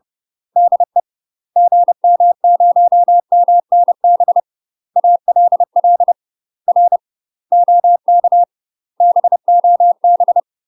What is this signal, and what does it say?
Morse code, heard as a keyed tone, about 30 wpm, 700 Hz. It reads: DE GM0MNB ALL R OK BOB
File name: UK MASTER QSOs TRAINING / 30WPM / UK MASTER QSOs 0003